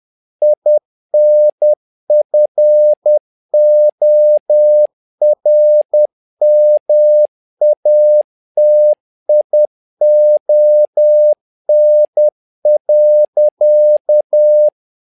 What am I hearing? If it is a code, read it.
Morse code, heard as a keyed tone, about 10 wpm, 600 Hz. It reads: INFORMATION.
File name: learning center / tower trouble / tower trouble 10wpm